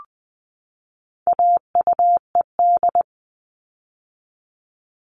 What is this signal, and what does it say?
Morse code, heard as a keyed tone, about 20 wpm, 700 Hz. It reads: AUED
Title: All Cut Numbers - 4-Digit 20wpm